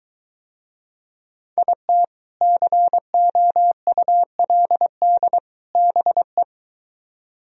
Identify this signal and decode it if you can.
Morse code, heard as a keyed tone, about 23 wpm, 700 Hz. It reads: IT COULD BE